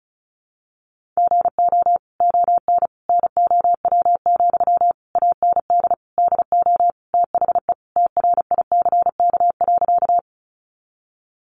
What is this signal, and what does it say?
Morse code, heard as a keyed tone, about 34 wpm, 700 Hz. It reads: GO ON NOW, AND DO THE TRICK.
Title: Sentences from Top 1500 Words 35wpm